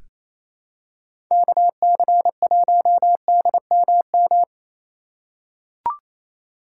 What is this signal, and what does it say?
Morse code, heard as a keyed tone, about 28 wpm, 700 Hz. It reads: KC1DMM
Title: USA Call Signs - 6x Character Spacing - 28wpm